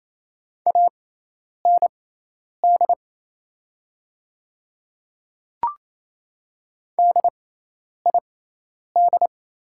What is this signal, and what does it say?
Morse code, heard as a keyed tone, about 28 wpm, spaced as for 10 wpm, 700 Hz. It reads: AND DID
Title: 6x Character Spacing - Top 100 Words - 28wpm